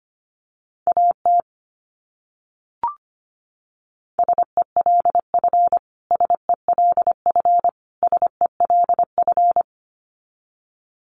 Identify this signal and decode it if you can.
Morse code, heard as a keyed tone, about 25 wpm, 700 Hz. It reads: AT SELF SELF SELF
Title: Top 200 Words Repeated 25wpm